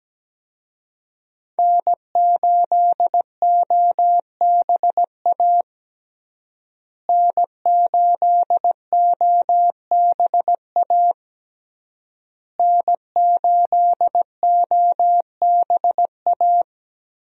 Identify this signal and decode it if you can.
Morse code, heard as a keyed tone, about 17 wpm, 700 Hz. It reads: N8OBA N8OBA N8OBA
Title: USA Call Signs - 5 Characters - Repeated 3x - 17wpm